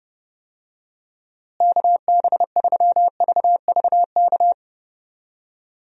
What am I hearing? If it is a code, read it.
Morse code, heard as a keyed tone, about 30 wpm, 700 Hz. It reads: KB3VVK
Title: USA Call Signs - 6 Characters - Repeated 3x - 30wpm